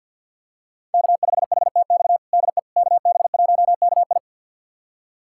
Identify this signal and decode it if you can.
Morse code, heard as a keyed tone, about 48 wpm, 700 Hz. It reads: K5HTX DE KB1KI
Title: USA Callsign DE Callsign - 5 Characters - 50wpm